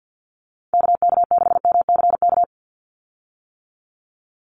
Morse code, heard as a keyed tone, about 48 wpm, 700 Hz. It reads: KK6MCK
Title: USA Call Signs 50wpm